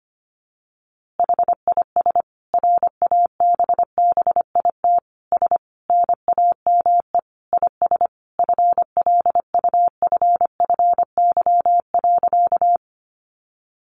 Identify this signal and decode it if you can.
Morse code, heard as a keyed tone, about 25 wpm, 700 Hz. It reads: HIS RABBIT S NAME IS FLUFFY.
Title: Easy Phrases 25wpm